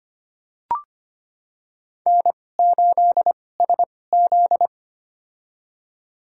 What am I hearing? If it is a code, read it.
Morse code, heard as a keyed tone, about 25 wpm, spaced as for 18 wpm, 700 Hz. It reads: N8SZ
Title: USA Call Signs - 4 Characters - 2x Character Spacing - 25wpm